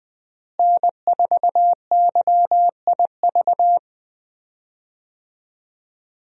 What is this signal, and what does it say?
Morse code, heard as a keyed tone, about 20 wpm, 700 Hz. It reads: N4YIV
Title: USA Call Signs - 5 Characters - Repeated 2x - 20wpm